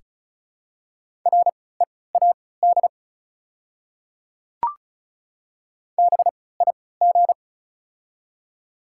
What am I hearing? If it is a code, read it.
Morse code, heard as a keyed tone, about 34 wpm, spaced as for 20 wpm, 700 Hz. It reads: READ BIG
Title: 3x Character Spacing - Top 300 Words - 35wpm